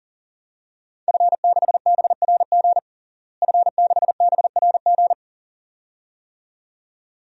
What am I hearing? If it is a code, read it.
Morse code, heard as a keyed tone, about 40 wpm, 700 Hz. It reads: F6BRG F6BRG
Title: DXCC-100 Call Signs - 5 Characters - Repeated 2x - 40wpm